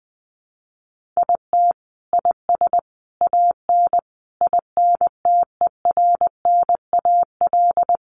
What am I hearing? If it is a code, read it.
Morse code, heard as a keyed tone, about 20 wpm, 700 Hz. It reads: IT IS AN INTERNAL
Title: Sentences from Top 1400 Words 20wpm